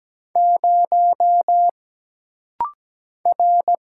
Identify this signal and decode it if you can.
Morse code, heard as a keyed tone, about 17 wpm, 700 Hz. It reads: 0 R
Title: Single Letter-Number - Rapid-Fire 17wpm